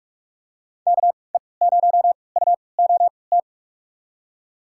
Morse code, heard as a keyed tone, about 44 wpm, spaced as for 25 wpm, 700 Hz. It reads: KE0UOT
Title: USA Call Signs - 6 Characters - 3x Character Spacing - 45wpm